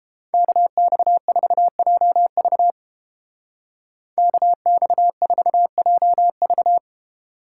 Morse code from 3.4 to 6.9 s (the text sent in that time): KX4JV